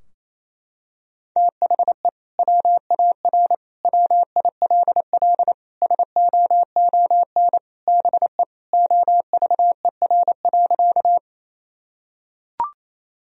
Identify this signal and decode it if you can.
Morse code, heard as a keyed tone, about 28 wpm, 700 Hz. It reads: THE WAR WILL SOON BE OVER.
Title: Sentences from Top 1000 Words - Review Part 2 - 28wpm